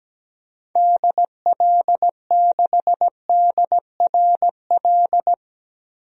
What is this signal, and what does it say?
Morse code, heard as a keyed tone, about 17 wpm, 700 Hz. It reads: DL6DRL
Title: DXCC-100 Call Signs - 6 Characters - 17wpm